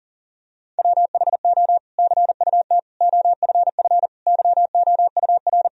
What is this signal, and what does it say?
Morse code, heard as a keyed tone, about 37 wpm, 700 Hz. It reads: WHO CUT OFF YOUR
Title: Sentences from Top 1000 Words - Review Part 1 - 40wpm